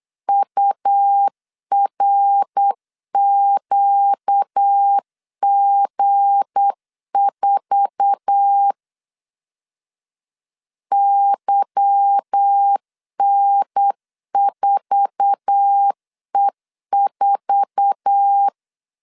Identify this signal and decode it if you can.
Morse code, heard as a keyed tone, about 8 wpm, 800 Hz. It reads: URQG4 YN4E4
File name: CW / DET / 1 9 1500C S C 8wpm 5653